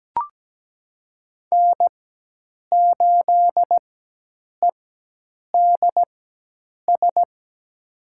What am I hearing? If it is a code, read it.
Morse code, heard as a keyed tone, about 17 wpm, spaced as for 8 wpm, 700 Hz. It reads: N8EDS
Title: USA Call Signs - 4x Character Spacing - 17wpm